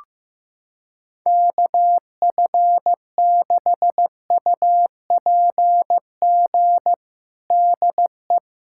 Morse code, heard as a keyed tone, about 16 wpm, 700 Hz. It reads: KF6UPG DE
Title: USA Callsign DE Callsign - 6 Characters - 15wpm